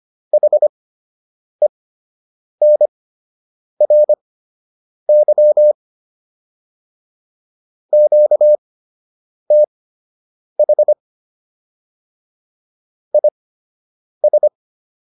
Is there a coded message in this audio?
Yes: Morse code from 0.3 to 14.5 s, heard as a keyed tone, about 25 wpm, spaced as for 8 wpm, 600 Hz. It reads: HENRY QTH IS